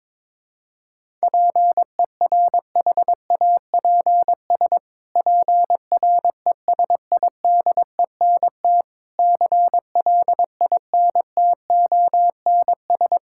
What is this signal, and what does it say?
Morse code, heard as a keyed tone, about 22 wpm, 700 Hz. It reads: PERHAPS PRESIDENT CLINTONS
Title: Pangram Challenge v2 - 22wpm